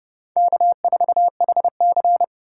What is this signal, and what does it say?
Morse code, heard as a keyed tone, about 38 wpm, 700 Hz. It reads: K4HC